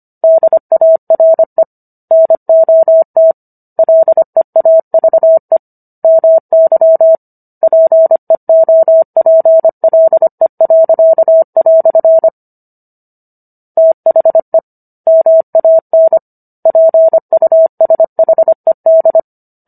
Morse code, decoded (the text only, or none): DARE NOT LEAVE MY PEOPLE." THE MAN PUSHED